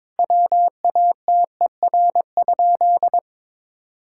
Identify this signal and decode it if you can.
Morse code, heard as a keyed tone, about 22 wpm, 700 Hz. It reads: WATER?
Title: Sentences from Top 1000 Words - Review Part 5 - 22wpm